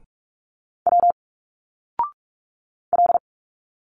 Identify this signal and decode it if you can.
Morse code, heard as a keyed tone, about 46 wpm, 700 Hz. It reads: W L
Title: Single Letters 45wpm